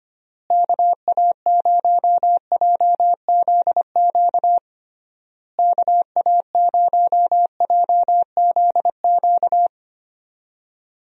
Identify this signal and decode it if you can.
Morse code, heard as a keyed tone, about 25 wpm, 700 Hz. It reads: KA0JZQ KA0JZQ
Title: USA Call Signs - 6 Characters - Repeated 3x - 25wpm